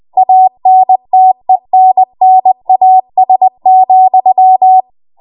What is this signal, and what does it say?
Morse code, heard as a keyed tone, about 20 wpm, 750 Hz. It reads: ANTENNAS,
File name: Morse / 20WPM